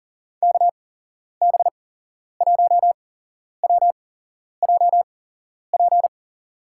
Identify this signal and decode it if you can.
Morse code, heard as a keyed tone, about 39 wpm, spaced as for 11 wpm, 700 Hz. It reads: KB1WJP
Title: USA Call Signs - 6 Characters - 8x Character Spacing - 40wpm